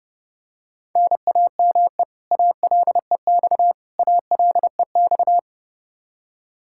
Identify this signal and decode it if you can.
Morse code, heard as a keyed tone, about 30 wpm, 700 Hz. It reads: NAME ALEX ALEX
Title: Practice QSO - Exchange 1 - Beginning - 30wpm